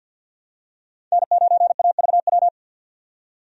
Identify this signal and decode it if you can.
Morse code, heard as a keyed tone, about 46 wpm, 700 Hz. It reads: N9AVW
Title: USA Call Signs - 5 Characters - 50wpm